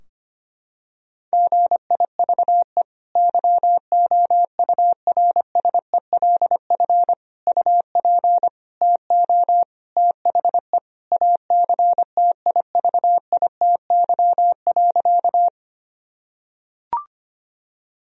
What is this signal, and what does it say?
Morse code, heard as a keyed tone, about 25 wpm, 700 Hz. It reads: GIVE YOURSELF UP TO THE ACTIVITY.
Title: Sentences from Top 1000 Words - Review Part 4 - 25wpm